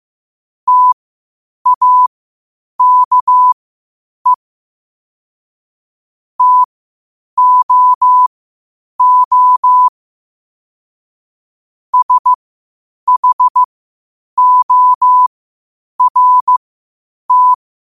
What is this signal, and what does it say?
Morse code, heard as a keyed tone, about 15 wpm, spaced as for 8 wpm, 1000 Hz. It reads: TAKE TOO SHORT
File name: qotdfiles / 8wpm